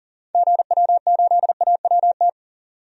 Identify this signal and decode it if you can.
Morse code, heard as a keyed tone, about 40 wpm, 700 Hz. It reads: GW8AWT
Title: DXCC-100 Call Signs - 6 Characters - 40wpm